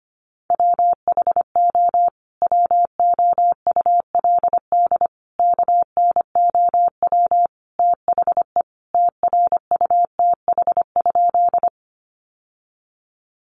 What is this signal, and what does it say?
Morse code, heard as a keyed tone, about 25 wpm, 700 Hz. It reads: WHO WOULD KNOW THE TRUTH?
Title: Sentences from Top 800 Words 25wpm